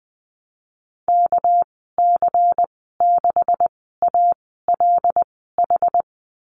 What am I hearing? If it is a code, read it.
Morse code, heard as a keyed tone, about 20 wpm, spaced as for 14 wpm, 700 Hz. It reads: KC6ALH
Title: 219 Lesson 35g - Number 0 - Call Signs 20wpm